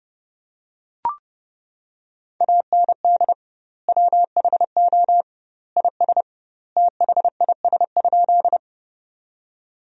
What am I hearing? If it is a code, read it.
Morse code, heard as a keyed tone, about 30 wpm, 700 Hz. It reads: AND WHO IS THIS?
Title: 2x Word-Spacing - Sentences from Top 100 Words - 30wpm